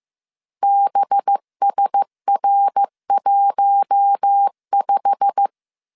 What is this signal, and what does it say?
Morse code, heard as a keyed tone, about 15 wpm, 800 Hz. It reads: BSR15
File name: CW / DET / 1 9 1500C S A 14wpm 3222